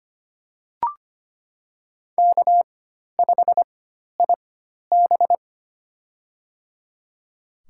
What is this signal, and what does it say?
Morse code, heard as a keyed tone, about 25 wpm, spaced as for 12 wpm, 700 Hz. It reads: K5IB
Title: USA Call Signs - 4 Characters - 4x Character Spacing - 25wpm